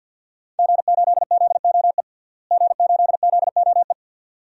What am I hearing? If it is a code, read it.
Morse code, heard as a keyed tone, about 46 wpm, 700 Hz. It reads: G8ZOE G8ZOE
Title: DXCC-100 Call Signs - 5 Characters - Repeated 2x - 50wpm